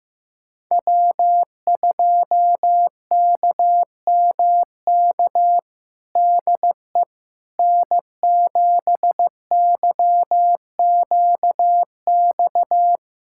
Morse code, heard as a keyed tone, about 16 wpm, 700 Hz. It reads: W2KMK DE N7YQX
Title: USA Callsign DE Callsign - 5 Characters - 15wpm